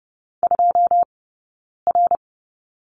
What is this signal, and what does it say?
Morse code, heard as a keyed tone, about 30 wpm, spaced as for 9 wpm, 690 Hz. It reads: 2R
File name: Lesson 18d - Letter m - Two Characters 30wpm